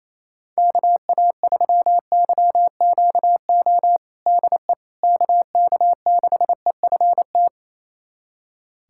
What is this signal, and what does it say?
Morse code, heard as a keyed tone, about 28 wpm, 700 Hz. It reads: KA3YQO DE KK6EFT
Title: USA Callsign DE Callsign - 6 Characters - 28wpm